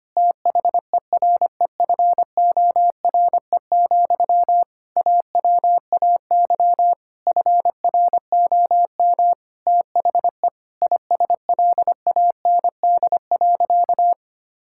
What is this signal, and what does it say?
Morse code, heard as a keyed tone, about 25 wpm, 700 Hz. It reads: THEREFORE, AWAY FROM THE ISLAND.
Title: Sentences from Top 1000 Words - Review Part 5 - 25wpm